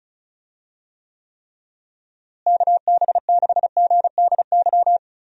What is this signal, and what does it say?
Morse code, heard as a keyed tone, about 34 wpm, 700 Hz. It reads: KB6GDY
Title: USA Call Signs - 6 Characters - Repeated 3x - 35wpm